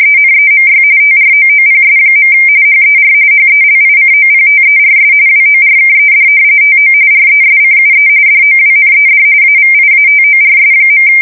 RTTY, about 45.45 baud. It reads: ENEMY TROOPS. HER HELICOPTERS WORKED TO RESCUE DOWNED AVIATORS.
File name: fsk170 / battleshipnj